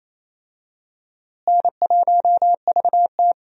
Morse code, heard as a keyed tone, about 28 wpm, 700 Hz. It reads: N1VT
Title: USA Call Signs - 4 Characters - Repeated 2x - 28wpm